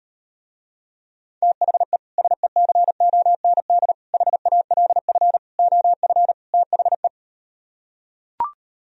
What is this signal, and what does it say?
Morse code, heard as a keyed tone, about 38 wpm, 700 Hz. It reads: THE SECOND HALF OF THE